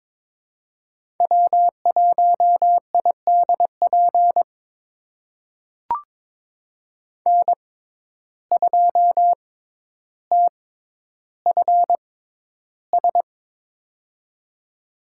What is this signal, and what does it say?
Morse code, heard as a keyed tone, about 23 wpm, spaced as for 11 wpm, 700 Hz. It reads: W1IDP N2TFS
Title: USA Call Signs - 5 Characters - 6x Character Spacing - 22wpm